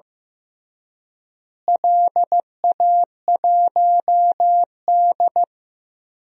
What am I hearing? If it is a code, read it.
Morse code, heard as a keyed tone, about 15 wpm, 700 Hz. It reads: LA1D
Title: DXCC-100 Call Signs - 4 Characters - Repeated 3x - 15wpm